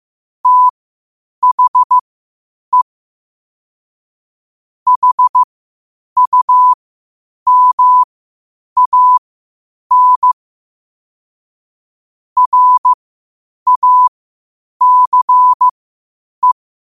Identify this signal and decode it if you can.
Morse code, heard as a keyed tone, about 15 wpm, spaced as for 8 wpm, 1000 Hz. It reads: THE HUMAN RACE
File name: qotdfiles / 8wpm